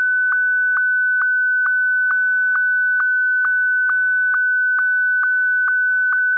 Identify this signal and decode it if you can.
SSTV (slow-scan television), Martin 1.
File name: martin1